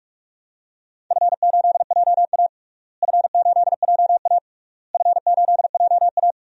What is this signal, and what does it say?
Morse code, heard as a keyed tone, about 45 wpm, 700 Hz. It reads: F8JA F8JA F8JA
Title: DXCC-100 Call Signs - 4 Characters - Repeated 3x - 45wpm